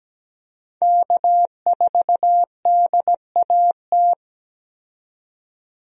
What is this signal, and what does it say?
Morse code, heard as a keyed tone, about 17 wpm, 700 Hz. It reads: K4DAT
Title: USA Call Signs - 5 Characters - 17wpm